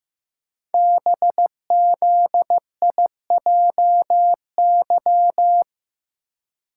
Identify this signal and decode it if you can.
Morse code, heard as a keyed tone, about 15 wpm, 700 Hz. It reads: BZIJY
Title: Code Groups - 5 Letters - 15wpm